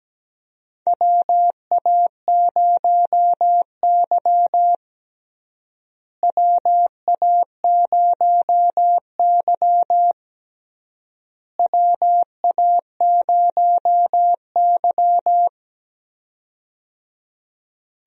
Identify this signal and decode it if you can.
Morse code, heard as a keyed tone, about 17 wpm, 700 Hz. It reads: WA0Y WA0Y WA0Y
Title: USA Call Signs - 4 Characters - Repeated 3x - 17wpm